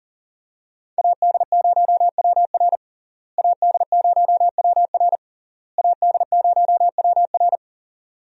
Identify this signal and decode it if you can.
Morse code, heard as a keyed tone, about 40 wpm, 700 Hz. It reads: AD0WR AD0WR AD0WR
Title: USA Call Signs - 5 Characters - Repeated 3x - 40wpm